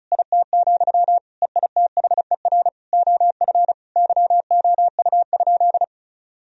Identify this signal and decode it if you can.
Morse code, heard as a keyed tone, about 36 wpm, 700 Hz. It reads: IT, EITHER OF YOU?
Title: Sentences from Top 1000 Words - Review Part 4 - 35wpm